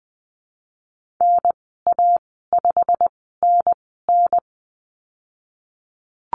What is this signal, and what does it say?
Morse code, heard as a keyed tone, about 20 wpm, spaced as for 15 wpm, 700 Hz. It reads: NA5NN
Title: 088 Lesson 15g - Number 5 - Call Signs 20wpm